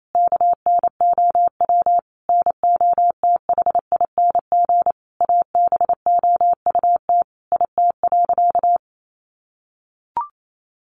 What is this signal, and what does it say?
Morse code, heard as a keyed tone, about 28 wpm, 700 Hz. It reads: KNOW NOTHING ABOUT IT.
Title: Sentences from Top 500 Words 28wpm